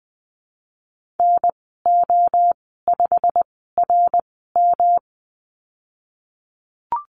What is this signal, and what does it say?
Morse code, heard as a keyed tone, about 20 wpm, spaced as for 15 wpm, 700 Hz. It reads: NO5RM
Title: 154 Lesson 25g - Letter p - Call Signs 20wpm